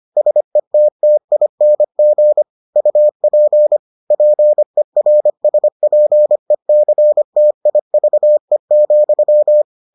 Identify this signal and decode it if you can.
Morse code, heard as a keyed tone, about 25 wpm, 600 Hz. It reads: SETTING UP PERSPECTIVE,